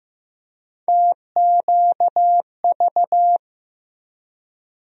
Morse code, heard as a keyed tone, about 16 wpm, 700 Hz. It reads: TQV
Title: USA Call Sign Postfixes - 3 Letters 15wpm